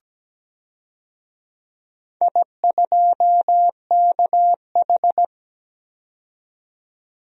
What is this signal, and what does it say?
Morse code, heard as a keyed tone, about 17 wpm, 700 Hz. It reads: I2KH
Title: DXCC-100 Call Signs - 4 Characters - 17wpm